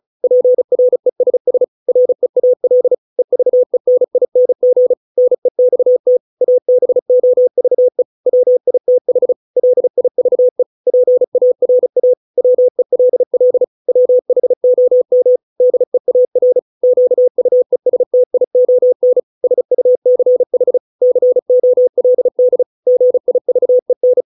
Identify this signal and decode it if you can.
Morse code, heard as a keyed tone, about 35 wpm, 500 Hz. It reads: PRESS REAL EVENING NEXT ABOVE WITH LIVE PARA WELL WHOM DEAR QUESTION SUCH COLD GIVEN